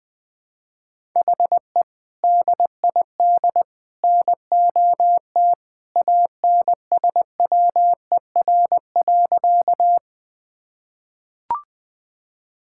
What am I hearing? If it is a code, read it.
Morse code, heard as a keyed tone, about 21 wpm, 700 Hz. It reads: HE DID NOT ANSWER.
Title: Sentences with 3-Word Phrases - 20wpm